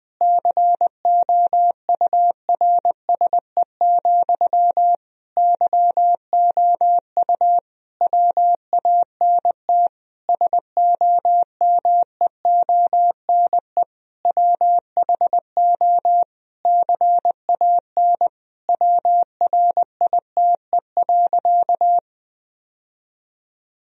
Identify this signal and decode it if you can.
Morse code, heard as a keyed tone, about 20 wpm, 700 Hz. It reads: COURSE, YOU WANT SOMEONE WHO CAN WRITE.
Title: Sentences from Top 1000 Words - Review Part 1 - 20wpm